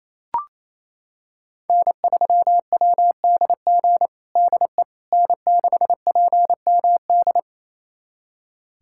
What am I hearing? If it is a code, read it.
Morse code, heard as a keyed tone, about 28 wpm, 700 Hz. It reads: N3WDG DE N6PMD
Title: USA Callsign DE Callsign - 5 Characters - 28wpm